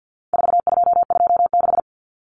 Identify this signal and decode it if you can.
Morse code, heard as a keyed tone, about 50 wpm, 700 Hz. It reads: 4226
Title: Common Cut Numbers - 4-Digit 50wpm